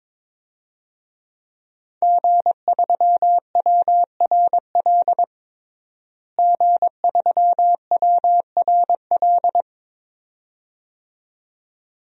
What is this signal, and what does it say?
Morse code, heard as a keyed tone, about 22 wpm, 700 Hz. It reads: G3WRL G3WRL
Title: DXCC-100 Call Signs - 5 Characters - Repeated 2x - 22wpm